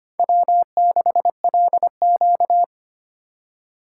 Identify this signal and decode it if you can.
Morse code, heard as a keyed tone, about 25 wpm, 700 Hz. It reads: W6LQ
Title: USA Call Signs - 4 Characters - 4x Character Spacing - 25wpm